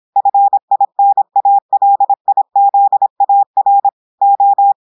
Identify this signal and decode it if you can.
Morse code, heard as a keyed tone, about 26 wpm, 800 Hz. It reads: FINALIZAR O
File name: portugais texte en clair 2 vitesse 26 mots minutes